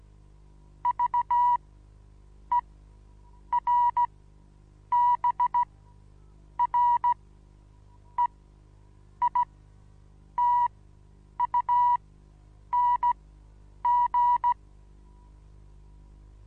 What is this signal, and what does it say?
Morse code, heard as a keyed tone, about 15 wpm, spaced as for 7 wpm, 970 Hz. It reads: VERBREITUNG
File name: Onlinelehrgang / morsen / DJ4UF Morsen25